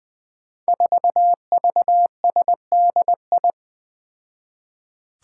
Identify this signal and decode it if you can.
Morse code, heard as a keyed tone, about 20 wpm, 700 Hz. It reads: 4VSDI
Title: Five Challenging Letter-Number Combinations 20wpm